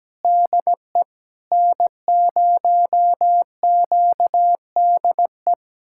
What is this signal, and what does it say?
Morse code, heard as a keyed tone, about 17 wpm, 700 Hz. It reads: DE N0QDE
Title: USA Callsign DE Callsign 17wpm